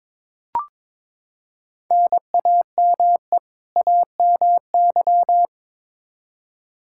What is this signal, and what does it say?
Morse code, heard as a keyed tone, about 22 wpm, 700 Hz. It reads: NAME AMY
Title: Practice QSO - Exchange 1 - Beginning - 22wpm